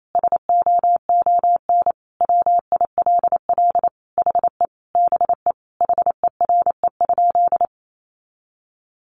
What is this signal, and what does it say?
Morse code, heard as a keyed tone, about 28 wpm, 700 Hz. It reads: SOON WILL HE BE HERE?
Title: Sentences from Top 400 Words 28wpm